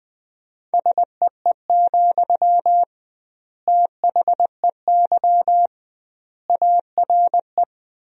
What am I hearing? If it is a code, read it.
Morse code, heard as a keyed tone, about 20 wpm, 700 Hz. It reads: SEE, THEY ARE
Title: 2x Word-Spacing - Sentences from Top 200 Words - 20wpm